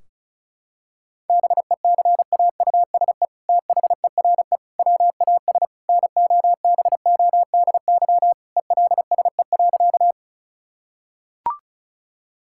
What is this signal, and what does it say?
Morse code, heard as a keyed tone, about 34 wpm, 700 Hz. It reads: BECAUSE THERE WAS NOBODY ELSE.
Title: Sentences from Top 1600 Words 35wpm